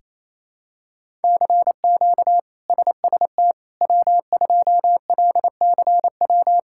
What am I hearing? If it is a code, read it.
Morse code, heard as a keyed tone, about 27 wpm, 700 Hz. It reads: CQ SST W2LCW
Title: K1USN SST QSO Practice - 28wpm